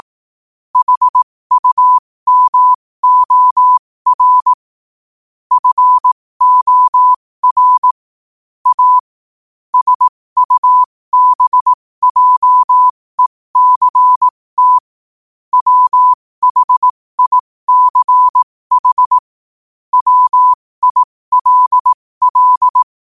Morse code, heard as a keyed tone, about 18 wpm, spaced as for 15 wpm, 1000 Hz. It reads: HUMOR FOR A SUBJECT WHICH WILL
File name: qotdfiles / 15wpm